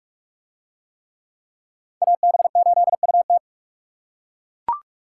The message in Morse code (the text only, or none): AB8UT